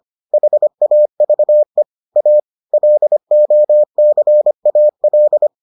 Morse code, heard as a keyed tone, about 25 wpm, 600 Hz. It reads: HAVE A LOCAL